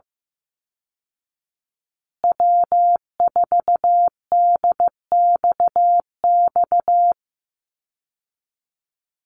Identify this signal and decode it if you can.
Morse code, heard as a keyed tone, about 15 wpm, 700 Hz. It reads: W4DXX
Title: Call Signs Repeated 2x - 15wpm